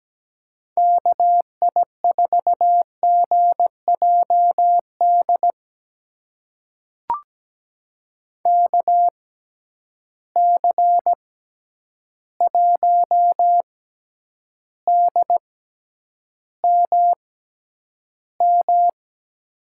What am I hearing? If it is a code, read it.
Morse code, heard as a keyed tone, about 17 wpm, spaced as for 9 wpm, 700 Hz. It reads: KI4GJD KC1DMM
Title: USA Call Signs - 6 Characters - 6x Character Spacing - 17wpm